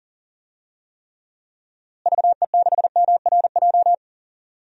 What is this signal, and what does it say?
Morse code, heard as a keyed tone, about 40 wpm, 700 Hz. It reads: VE6MRJ
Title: DXCC-100 Call Signs - 6 Characters - Repeated 2x - 40wpm